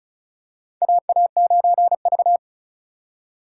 Morse code, heard as a keyed tone, about 36 wpm, 700 Hz. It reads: AA9V
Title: USA Call Signs - 4 Characters - 3x Character Spacing - 35wpm